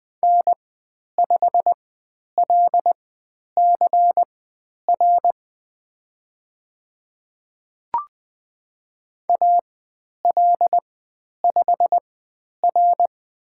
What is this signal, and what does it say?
Morse code, heard as a keyed tone, about 20 wpm, spaced as for 10 wpm, 700 Hz. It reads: N5LCR AL5R
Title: 102 Lesson 17g - Letter u - Call Signs 20wpm